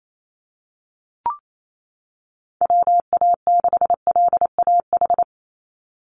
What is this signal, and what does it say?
Morse code, heard as a keyed tone, about 28 wpm, 700 Hz. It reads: WA6LAH
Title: USA Call Signs 28wpm